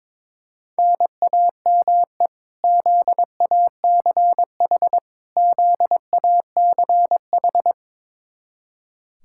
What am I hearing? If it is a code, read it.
Morse code, heard as a keyed tone, about 21 wpm, 700 Hz. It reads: NAME ZACH ZACH
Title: Practice QSO - Exchange 1 - Beginning - 22wpm